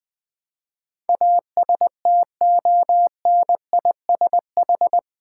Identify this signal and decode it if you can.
Morse code, heard as a keyed tone, about 20 wpm, 700 Hz. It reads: ASTONISH
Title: Build A Word - 4 Letters At A Time - 11-Letter Words - 20wpm